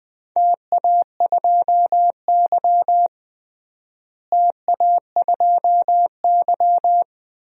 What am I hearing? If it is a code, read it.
Morse code, heard as a keyed tone, about 20 wpm, 700 Hz. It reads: TA2Y TA2Y
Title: DXCC-100 Call Signs - 4 Characters - Repeated 2x - 20wpm